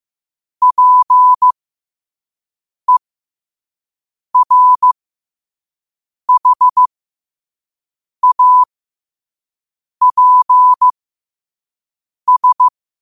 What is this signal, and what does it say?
Morse code, heard as a keyed tone, about 15 wpm, spaced as for 5 wpm, 1000 Hz. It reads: PERHAPS
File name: qotdfiles / 5wpm